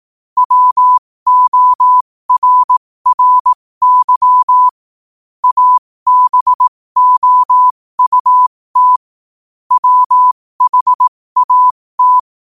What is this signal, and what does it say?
Morse code, heard as a keyed tone, about 18 wpm, spaced as for 15 wpm, 1000 Hz. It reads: WORRY ABOUT WHAT
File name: qotdfiles / 15wpm